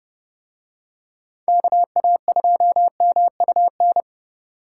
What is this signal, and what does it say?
Morse code, heard as a keyed tone, about 30 wpm, 700 Hz. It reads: KA2MUN